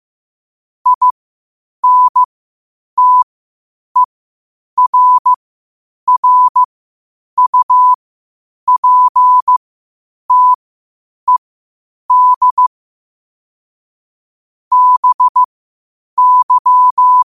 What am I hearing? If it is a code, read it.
Morse code, heard as a keyed tone, about 15 wpm, spaced as for 8 wpm, 1000 Hz. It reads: INTERRUPTED BY
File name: 8wpm